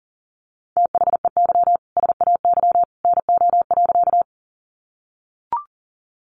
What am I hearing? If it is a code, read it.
Morse code, heard as a keyed tone, about 40 wpm, 700 Hz. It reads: THEY SAY NO.
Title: Easy Phrases 40wpm